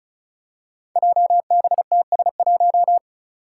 Morse code, heard as a keyed tone, about 36 wpm, 700 Hz. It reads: JBTS1
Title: Five Challenging Letter-Number Combinations 35wpm